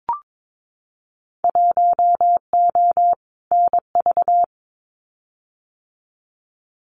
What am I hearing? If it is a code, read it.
Morse code, heard as a keyed tone, about 22 wpm, 700 Hz. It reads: 1O NV
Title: Winter Field Day Exchanges Encore 22wpm